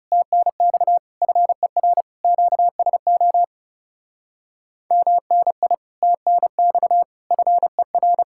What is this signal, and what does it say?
Morse code, heard as a keyed tone, about 37 wpm, 700 Hz. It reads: TNX FER QSO MNI TNX FER
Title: Speed-Racing Common QSO Phrases Encore - 30wpm